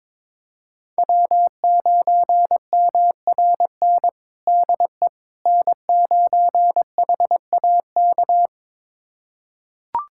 Morse code, heard as a keyed tone, about 22 wpm, 700 Hz. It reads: W9MRN DE N9HAK
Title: USA Callsign DE Callsign - 5 Characters - 22wpm